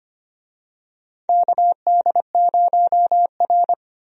Morse code, heard as a keyed tone, about 25 wpm, 700 Hz. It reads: KD0R
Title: USA Call Signs - 4 Characters - 25wpm